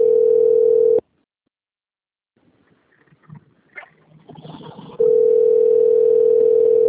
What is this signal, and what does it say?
Ringback tone (North America), 440 + 480 Hz.
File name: rg-1600-unknown- - - .7644